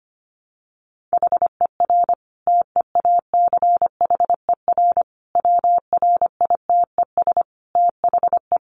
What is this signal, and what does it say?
Morse code, heard as a keyed tone, about 25 wpm, 700 Hz. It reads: HER TEACHER WRITES THE
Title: Additional Easy Phrases 25wpm